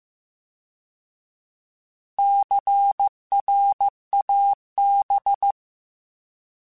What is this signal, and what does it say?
Morse code, heard as a keyed tone, about 15 wpm, 800 Hz. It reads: CRAB